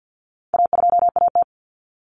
Morse code, heard as a keyed tone, about 50 wpm, 700 Hz. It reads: A2AT
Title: Common Cut Numbers - 4-Digit 50wpm